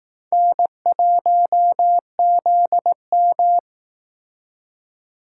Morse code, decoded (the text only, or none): N1ZM